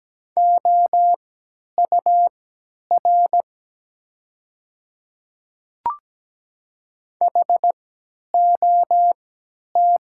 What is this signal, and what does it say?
Morse code, heard as a keyed tone, about 17 wpm, spaced as for 10 wpm, 700 Hz. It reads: OUR HOT
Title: 3x Character Spacing - Top 300 Words - 17wpm